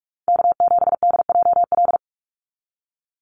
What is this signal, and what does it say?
Morse code, heard as a keyed tone, about 44 wpm, 700 Hz. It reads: K7DJL
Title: NA SOTA Chaser Callsigns 45wpm